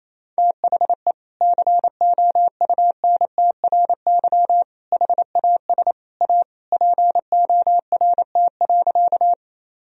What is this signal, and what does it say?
Morse code, heard as a keyed tone, about 28 wpm, 700 Hz. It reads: THE COUNTRY HAS A PORT.
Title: Sentences from Top 1000 Words - Review Part 1 - 28wpm